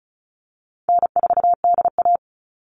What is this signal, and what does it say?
Morse code, heard as a keyed tone, about 34 wpm, 700 Hz. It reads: N4DA
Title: NA SOTA Chaser Callsigns 35wpm